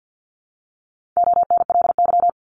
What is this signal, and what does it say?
Morse code, heard as a keyed tone, about 54 wpm, 700 Hz. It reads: ONLY
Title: Top 500 Words Repeated 50wpm